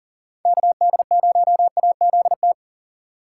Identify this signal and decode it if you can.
Morse code, heard as a keyed tone, about 48 wpm, 700 Hz. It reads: KD0AZT
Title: Speed Racing - USA 2x3 Call Signs 40wpm